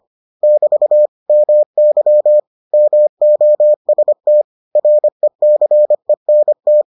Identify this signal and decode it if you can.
Morse code, heard as a keyed tone, about 25 wpm, 600 Hz. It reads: <BT> MY MOST RECENT